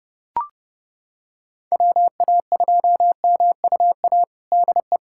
Morse code, heard as a keyed tone, about 30 wpm, 700 Hz. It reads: WA2MUA DE
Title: USA Callsign DE Callsign 30wpm